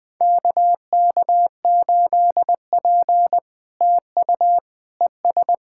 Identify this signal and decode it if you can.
Morse code, heard as a keyed tone, about 19 wpm, 700 Hz. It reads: KK8P TU ES
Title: POTA QSO Practice - 20wpm